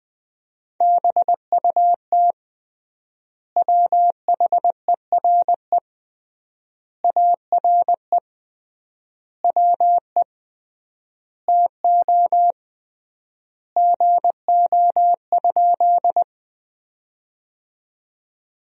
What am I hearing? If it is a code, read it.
Morse code, heard as a keyed tone, about 21 wpm, 700 Hz. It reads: BUT WHERE ARE WE TO GO?
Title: 3x Word-Spacing - Sentences from Top 200 Words - 20wpm